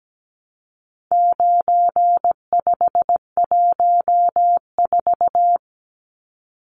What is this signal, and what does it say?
Morse code, heard as a keyed tone, about 17 wpm, 700 Hz. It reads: 9514
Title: Numbers - 4-Digit 17wpm